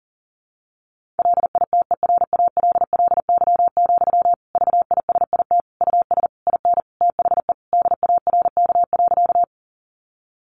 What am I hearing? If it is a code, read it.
Morse code, heard as a keyed tone, about 40 wpm, 700 Hz. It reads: LITERALLY, VISIT US IN THE DARK.
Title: Sentences from Top 1400 Words 40wpm